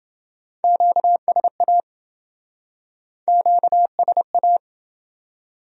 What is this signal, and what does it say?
Morse code, heard as a keyed tone, about 28 wpm, 700 Hz. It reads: QSA QSA